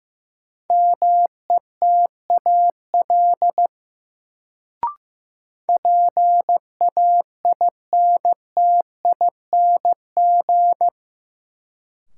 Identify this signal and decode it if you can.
Morse code, heard as a keyed tone, about 15 wpm, 700 Hz. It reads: METAL PAINTING
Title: Sentences from Top 1000 Words - Review Part 6 - New Word Drill - 15wpm